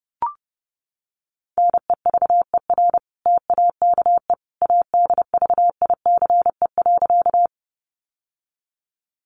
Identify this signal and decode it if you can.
Morse code, heard as a keyed tone, about 30 wpm, 700 Hz. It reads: NEVER TAKE ADVICE.
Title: Sentences from Top 1100 Words 30wpm